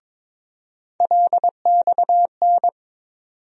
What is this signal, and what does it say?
Morse code, heard as a keyed tone, about 22 wpm, 700 Hz. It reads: LXN
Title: USA Call Sign Postfixes - 3 Letters 22wpm